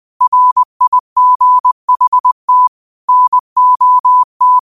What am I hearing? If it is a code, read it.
Morse code, heard as a keyed tone, about 20 wpm, 1000 Hz. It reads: RIGHT NOT